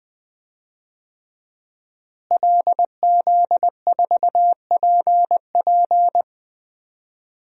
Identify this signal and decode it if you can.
Morse code, heard as a keyed tone, about 20 wpm, 700 Hz. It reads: LZ4PP
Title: DXCC-100 Call Signs - 5 Characters - Repeated 3x - 20wpm